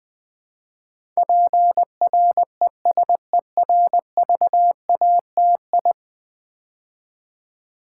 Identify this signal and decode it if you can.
Morse code, heard as a keyed tone, about 20 wpm, 700 Hz. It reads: PRESERVATI
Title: Build A Word - 5 Letters At A Time - 12-Letter Words - 20wpm